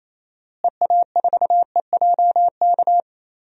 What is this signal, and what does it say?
Morse code, heard as a keyed tone, about 28 wpm, 700 Hz. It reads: EA4EJK
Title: DXCC-100 Call Signs - 6 Characters - Repeated 2x - 28wpm